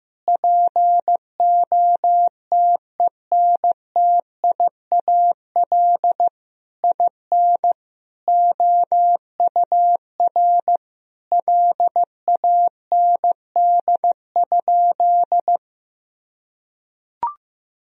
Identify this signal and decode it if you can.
Morse code, heard as a keyed tone, about 15 wpm, 700 Hz. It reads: POTENTIAL IN OUR LAND?
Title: Sentences from Top 1000 Words - Review Part 4 - 15wpm